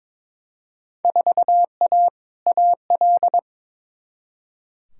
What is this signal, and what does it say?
Morse code, heard as a keyed tone, about 22 wpm, 700 Hz. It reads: 4A AL
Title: Field Day Exchanges - 22wpm